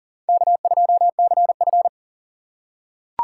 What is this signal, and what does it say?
Morse code, heard as a keyed tone, about 40 wpm, 700 Hz. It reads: K2CF
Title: USA Call Signs - 4 Characters - 4x Character Spacing - 40wpm